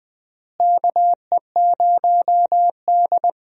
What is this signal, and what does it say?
Morse code, heard as a keyed tone, about 20 wpm, 700 Hz. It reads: KE0D
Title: USA Call Signs - 4 Characters - Repeated 3x - 20wpm